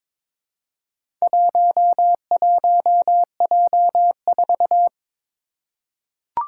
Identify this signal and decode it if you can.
Morse code, heard as a keyed tone, about 23 wpm, 700 Hz. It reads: 11J4
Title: Four Challenging Letter-Number Combinations 22wpm